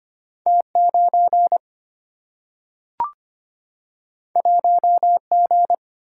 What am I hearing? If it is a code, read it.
Morse code, heard as a keyed tone, about 25 wpm, 700 Hz. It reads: T9 1G
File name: Two Letter-Number Combinations - 25wpm